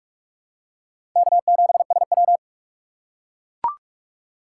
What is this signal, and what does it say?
Morse code, heard as a keyed tone, about 45 wpm, 700 Hz. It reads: K7SW
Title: USA Call Signs - 4 Characters - 3x Character Spacing - 45wpm